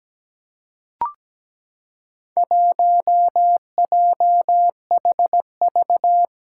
Morse code, heard as a keyed tone, about 17 wpm, 700 Hz. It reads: 1JHV
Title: Four Challenging Letter-Number Combinations 17wpm